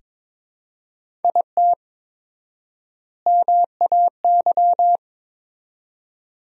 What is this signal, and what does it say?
Morse code, heard as a keyed tone, about 22 wpm, 700 Hz. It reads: IT MAY
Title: 4x Word-Spacing - Sentences from Top 400 Words - 22wpm